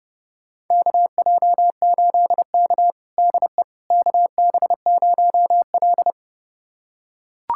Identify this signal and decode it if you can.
Morse code, heard as a keyed tone, about 32 wpm, 700 Hz. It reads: KJ8K DE KB0L
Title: USA Callsign DE Callsign - 4 Characters - 30wpm